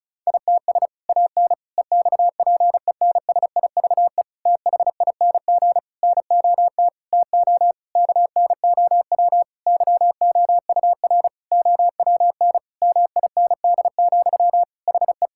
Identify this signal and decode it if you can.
Morse code, heard as a keyed tone, about 36 wpm, 700 Hz. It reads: ITS AN EXPENSIVE THING NOT TO KNOW YOUR OWN MIND, HE
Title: Sentences from Top 1000 Words - Review Part 6 - 35wpm